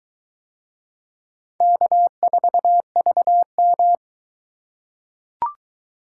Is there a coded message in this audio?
Yes: Morse code from 1.6 to 4.0 s, heard as a keyed tone, about 20 wpm, 700 Hz. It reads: K4VM